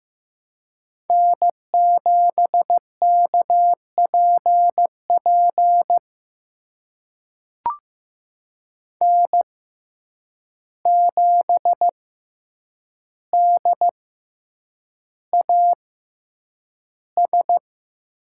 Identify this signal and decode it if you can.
Morse code, heard as a keyed tone, about 15 wpm, spaced as for 8 wpm, 700 Hz. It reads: N7KPP N7DAS
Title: USA Call Signs - 5 Characters - 6x Character Spacing - 15wpm